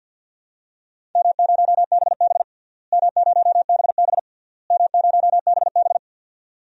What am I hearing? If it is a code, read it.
Morse code, heard as a keyed tone, about 45 wpm, 700 Hz. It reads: M0BB M0BB M0BB
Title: DXCC-100 Call Signs - 4 Characters - Repeated 3x - 50wpm